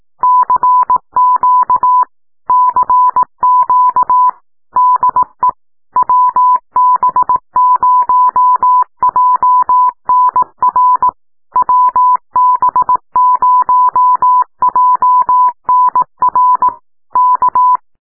Morse code, heard as a keyed tone, about 18 wpm, 1000 Hz. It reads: CQ CQ DE WB0JNR WB0JNR K
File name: hearcw / cw cq by wb0jnr